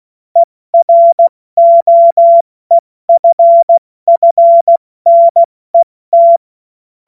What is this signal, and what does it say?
Morse code, heard as a keyed tone, about 16 wpm, 680 Hz. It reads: EROEFFNET
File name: Morsekurs / Witze / Witz 10. Mai 2022 70 F80